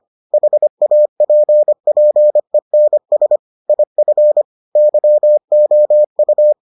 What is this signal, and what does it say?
Morse code, heard as a keyed tone, about 25 wpm, 600 Hz. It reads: HAPPENS IF YOU